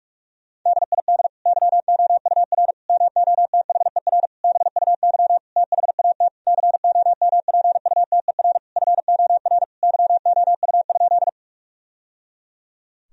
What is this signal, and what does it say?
Morse code, heard as a keyed tone, about 47 wpm, 700 Hz. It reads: DID YOUR MOTHER BUY THAT COMPUTER FOR YOU?
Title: Sentences from Top 1000 Words - Review Part 3 - 45wpm